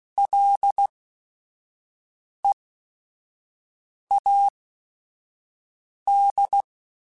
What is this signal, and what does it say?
Morse code, heard as a keyed tone, about 16 wpm, spaced as for 5 wpm, 800 Hz. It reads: LEAD
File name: Ls / Single L m3s / Single L @5